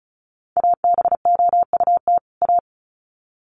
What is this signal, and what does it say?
Morse code, heard as a keyed tone, about 34 wpm, 700 Hz. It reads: ABOUT A
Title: Sets of 2 Words 35wpm